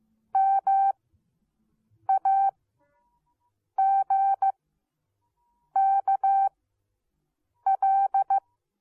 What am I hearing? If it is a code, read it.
Morse code, heard as a keyed tone, about 15 wpm, spaced as for 6 wpm, 780 Hz. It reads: MAGKL